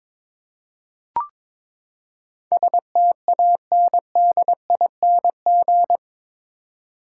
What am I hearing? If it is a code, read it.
Morse code, heard as a keyed tone, about 21 wpm, 700 Hz. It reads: STANDING
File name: practice / 22wpm / Eight Letter Words - 22wpm